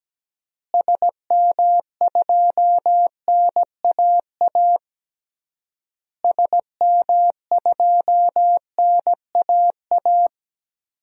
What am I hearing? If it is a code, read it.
Morse code, heard as a keyed tone, about 17 wpm, 700 Hz. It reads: SM2NAA SM2NAA
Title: DXCC-100 Call Signs - 6 Characters - Repeated 3x - 17wpm